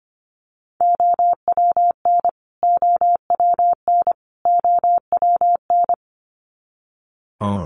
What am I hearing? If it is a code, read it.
Morse code, heard as a keyed tone, about 25 wpm, 700 Hz. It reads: OWN OWN OWN
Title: Top 300 Words Repeated 25wpm